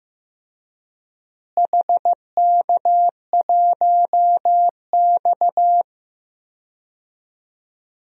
Morse code, heard as a keyed tone, about 15 wpm, 700 Hz. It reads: HK1X
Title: DXCC-100 Call Signs - 4 Characters - Repeated 2x - 15wpm